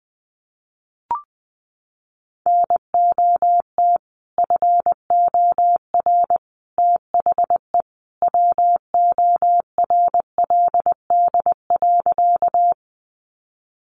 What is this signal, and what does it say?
Morse code, heard as a keyed tone, about 19 wpm, 700 Hz. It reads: NOT FOR THE WORLD.
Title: Sentences from Top 600 Words 20wpm